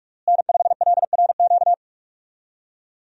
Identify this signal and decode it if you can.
Morse code, heard as a keyed tone, about 44 wpm, 700 Hz. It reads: N5LRQ
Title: USA Call Signs - 5 Characters - Repeated 2x - 45wpm